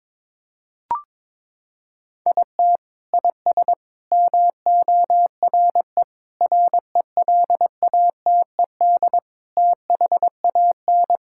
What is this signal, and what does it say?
Morse code, heard as a keyed tone, about 22 wpm, 700 Hz. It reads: IT IS MORE RELATED THAN
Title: Sentences from Top 1000 Words - Review Part 5 - 22wpm